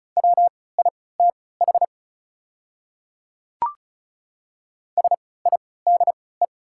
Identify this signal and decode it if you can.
Morse code, heard as a keyed tone, about 34 wpm, spaced as for 20 wpm, 700 Hz. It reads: WITH SIDE
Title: 3x Character Spacing - Top 200 Words - 35wpm